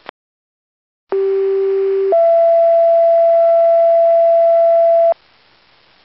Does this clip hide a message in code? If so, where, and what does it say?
Yes: two-tone page from 1.1 to 5.1 s, heard as a one-second tone, then a three-second tone, about 390 then 670 Hz.